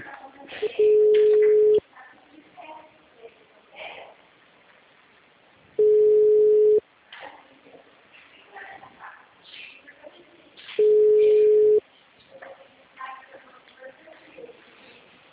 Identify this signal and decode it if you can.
Ringback tone (Europe, 425 Hz), with speech.
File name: out- -2003- - - .82318